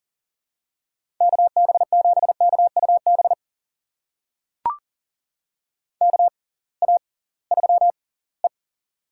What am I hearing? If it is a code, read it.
Morse code, heard as a keyed tone, about 40 wpm, spaced as for 23 wpm, 700 Hz. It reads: KB7KUB KA3E
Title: USA Call Signs - 6x Character Spacing - 40wpm